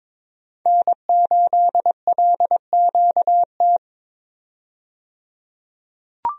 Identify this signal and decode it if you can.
Morse code, heard as a keyed tone, about 22 wpm, 700 Hz. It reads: N8LQT
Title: USA Call Signs - 5 Characters - 22wpm